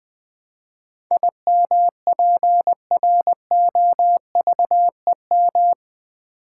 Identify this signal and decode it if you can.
Morse code, heard as a keyed tone, about 20 wpm, 700 Hz. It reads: IMPROVEM
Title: Build A Word - 4 Letters At A Time - 11-Letter Words - 20wpm